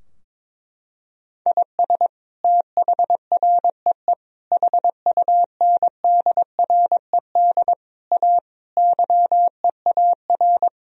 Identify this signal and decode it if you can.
Morse code, heard as a keyed tone, about 22 wpm, 700 Hz. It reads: IS THREE HUNDRED A YEAR
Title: Sentences from Top 1000 Words - Review Part 2 - 22wpm